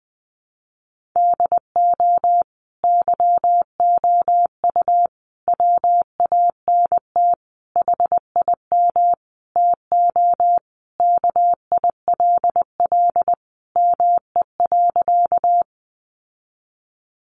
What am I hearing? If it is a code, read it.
Morse code, heard as a keyed tone, about 20 wpm, 700 Hz. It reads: DO YOU WANT HIM TO KILL ME.
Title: Sentences from Top 1400 Words 20wpm